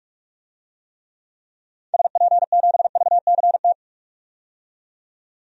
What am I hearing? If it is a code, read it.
Morse code, heard as a keyed tone, about 42 wpm, 700 Hz. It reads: SP7VCT
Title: DXCC-100 Call Signs - 6 Characters - 45wpm